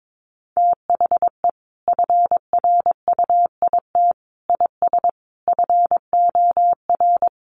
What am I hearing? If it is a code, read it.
Morse code, heard as a keyed tone, about 22 wpm, 700 Hz. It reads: THE FRUIT IS FOR
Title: Sentences from Top 1300 Words 22wpm